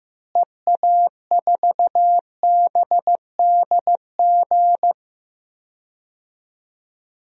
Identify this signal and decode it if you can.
Morse code, heard as a keyed tone, about 15 wpm, 700 Hz. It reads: EA4BDG
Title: DXCC-100 Call Signs - 6 Characters - Repeated 3x - 15wpm